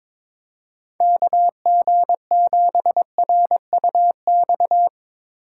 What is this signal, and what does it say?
Morse code, heard as a keyed tone, about 22 wpm, 700 Hz. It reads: KG7RUX
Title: USA Call Signs - 6 Characters - 3x Character Spacing - 22wpm